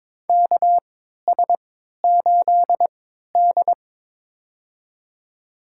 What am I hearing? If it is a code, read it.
Morse code, heard as a keyed tone, about 22 wpm, spaced as for 13 wpm, 700 Hz. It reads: KS8D
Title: USA Call Signs - 4 Characters - 3x Character Spacing - 22wpm